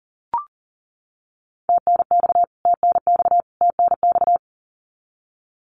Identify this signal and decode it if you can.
Morse code, heard as a keyed tone, about 40 wpm, 700 Hz. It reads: TNX TNX TNX
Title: Abbreviations Repeated 3x - 40wpm